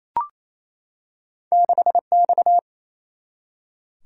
Morse code, heard as a keyed tone, about 28 wpm, 700 Hz. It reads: 6X
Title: Single-Letter Single-Number 28wpm